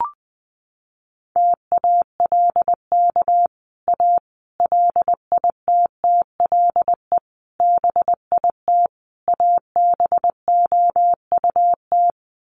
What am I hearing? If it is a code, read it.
Morse code, heard as a keyed tone, about 20 wpm, 700 Hz. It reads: TALK A LITTLE BIT ABOUT
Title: Sets of 5 Words Encore 20wpm